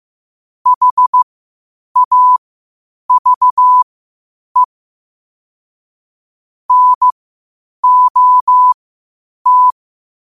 Morse code, heard as a keyed tone, about 15 wpm, spaced as for 8 wpm, 1000 Hz. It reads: HAVE NOT
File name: qotdfiles / 8wpm